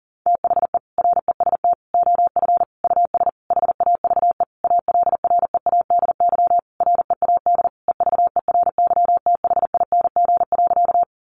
Morse code, heard as a keyed tone, about 40 wpm, 700 Hz. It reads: THE REST OF US HAVE ALREADY READ EVERYTHING.
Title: Sentences from Top 700 Words 40wpm